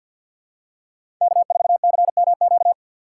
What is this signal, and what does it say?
Morse code, heard as a keyed tone, about 47 wpm, 700 Hz. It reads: K4CMQ
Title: USA Call Signs - 5 Characters - 4x Character Spacing - 50wpm